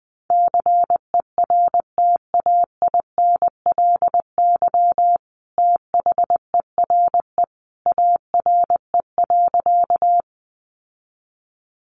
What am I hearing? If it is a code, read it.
Morse code, heard as a keyed tone, about 20 wpm, 700 Hz. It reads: CERTAINLY THERE ARE.
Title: Sentences from Top 900 Words 20wpm